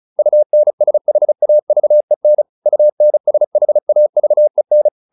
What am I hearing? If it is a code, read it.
Morse code, heard as a keyed tone, about 34 wpm, 600 Hz. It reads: UNSHAVEN UNSHAVEN